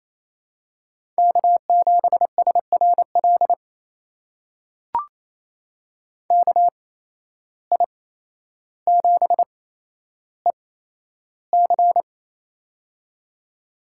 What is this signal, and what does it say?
Morse code, heard as a keyed tone, about 28 wpm, spaced as for 12 wpm, 700 Hz. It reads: K7SRL KI7EC
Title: USA Call Signs - 5 Characters - 8x Character Spacing - 28wpm